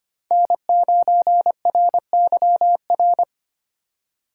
Morse code, heard as a keyed tone, about 25 wpm, 700 Hz. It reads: N9RYR
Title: USA Call Signs - 5 Characters - 3x Character Spacing - 25wpm